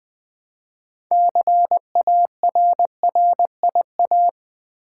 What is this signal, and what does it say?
Morse code, heard as a keyed tone, about 20 wpm, 700 Hz. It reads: CARRIA
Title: Build A Word - 2 Letters At A Time - 8-Letter Words - 20wpm